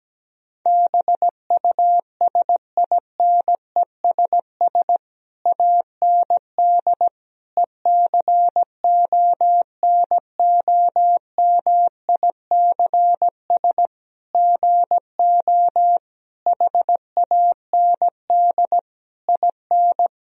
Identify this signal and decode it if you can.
Morse code, heard as a keyed tone, about 17 wpm, 700 Hz. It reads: BUSINESS AND ECONOMICS GO HAND IN
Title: Sentences from Top 1000 Words - Review Part 3 - 17wpm